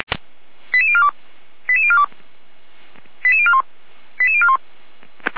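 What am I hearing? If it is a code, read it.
Five-tone selective call (ZVEI), heard as quick runs of five tones, tones of about 70 ms. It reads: 88042 88042 88041 88041
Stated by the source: ZVEI five-tone call 88253 received by SDR, recorded by Feuerwehr Orlamünde.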